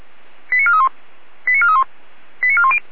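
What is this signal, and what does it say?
Five-tone selective call (ZVEI), heard as quick runs of five tones, tones of about 70 ms. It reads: 89431 89431 89410
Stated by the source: Five-tone ZVEI selective call received by SDR.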